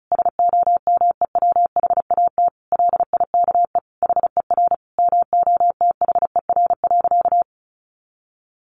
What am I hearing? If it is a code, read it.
Morse code, heard as a keyed tone, about 34 wpm, 700 Hz. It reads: SOMEWHAT LIKE HER MOTHER.
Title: Sentences from Top 1300 Words 35wpm